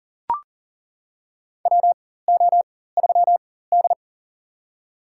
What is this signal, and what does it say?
Morse code, heard as a keyed tone, about 39 wpm, spaced as for 19 wpm, 700 Hz. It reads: WO3D
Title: USA Call Signs - 4x Character Spacing - 40wpm